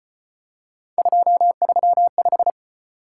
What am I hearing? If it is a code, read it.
Morse code, heard as a keyed tone, about 33 wpm, 700 Hz. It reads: 235